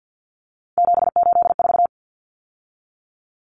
Morse code, heard as a keyed tone, about 50 wpm, 700 Hz. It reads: 784
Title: Common Cut Numbers - 3-Digit 50wpm